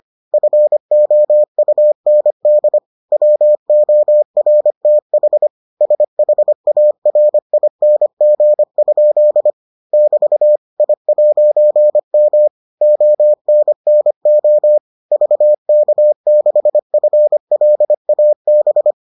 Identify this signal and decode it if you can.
Morse code, heard as a keyed tone, about 25 wpm, 600 Hz. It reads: FOUND WORTH SHARING? <BT> I'M ONNO VK6FLAB